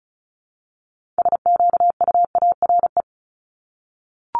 Morse code, heard as a keyed tone, about 36 wpm, 700 Hz. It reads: SQUARE
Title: Multi-letter Phonemes 35wpm